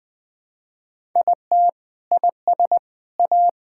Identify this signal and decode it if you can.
Morse code, heard as a keyed tone, about 21 wpm, 700 Hz. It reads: IT IS A
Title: Sentences from Top 1000 Words - Review Part 5 - 20wpm